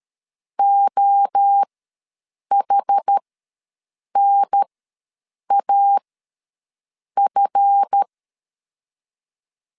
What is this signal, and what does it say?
Morse code, heard as a keyed tone, about 13 wpm, spaced as for 7 wpm, 800 Hz. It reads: OHNAF
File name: LEZIONI / 10 SIGNS / 10B INT ws 12wpm 3527